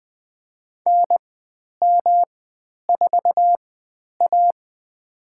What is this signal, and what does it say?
Morse code, heard as a keyed tone, about 20 wpm, spaced as for 10 wpm, 700 Hz. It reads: NM4A
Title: 109 Lesson 18g - Letter m - Call Signs 20wpm